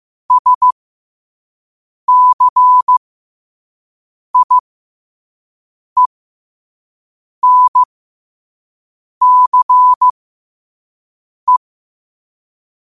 Morse code, heard as a keyed tone, about 15 wpm, spaced as for 5 wpm, 1000 Hz. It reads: SCIENCE
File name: qotdfiles / 5wpm